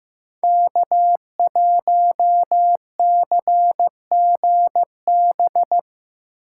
Morse code, heard as a keyed tone, about 15 wpm, 700 Hz. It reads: K1CGB
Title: Code Groups - 5 Letter:Numbers - 15wpm